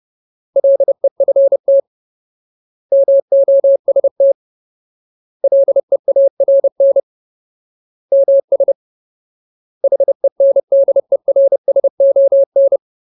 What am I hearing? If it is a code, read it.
Morse code, heard as a keyed tone, about 30 wpm, 550 Hz. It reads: LEFT MOST LEARN MS HENDERSON
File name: Ultimate Word List / 30WPM Ultimate